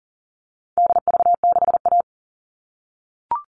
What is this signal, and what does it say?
Morse code, heard as a keyed tone, about 40 wpm, 700 Hz. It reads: DV6A
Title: All Cut Numbers - 4-Digit 40wpm